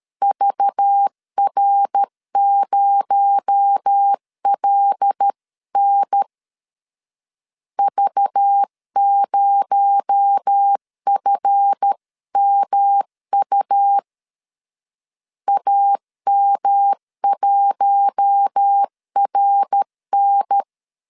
Morse code, heard as a keyed tone, about 13 wpm, 800 Hz. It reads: VR0LN V0FMU AM1RN
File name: LEZIONI / 7 GF9Z / 7 GF 12wpm 2122